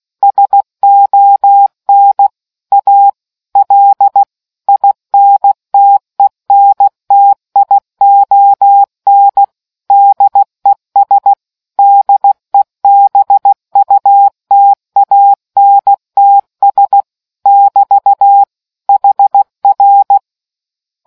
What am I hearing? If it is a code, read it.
Morse code, heard as a keyed tone, about 16 wpm, 800 Hz. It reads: SON A L INTENTION DES DEBUTANTS <BT> HR